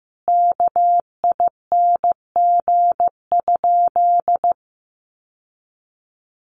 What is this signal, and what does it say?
Morse code, heard as a keyed tone, about 15 wpm, 700 Hz. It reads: KING?
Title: Sentences from Top 1400 Words 15wpm